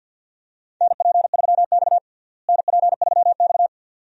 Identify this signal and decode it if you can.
Morse code, heard as a keyed tone, about 46 wpm, 700 Hz. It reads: NP3X NP3X
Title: USA Call Signs - 4 Characters - Repeated 2x - 50wpm